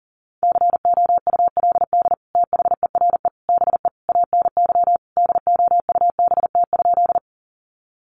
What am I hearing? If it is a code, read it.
Morse code, heard as a keyed tone, about 40 wpm, 700 Hz. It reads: COULD THERE BE ANY DOUBT?
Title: Sentences from Top 1500 Words 40wpm